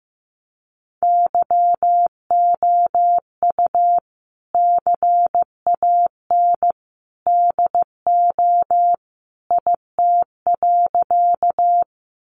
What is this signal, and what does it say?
Morse code, heard as a keyed tone, about 15 wpm, 700 Hz. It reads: YOU CAN DO IT.
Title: Sentences from Top 100 Words 15wpm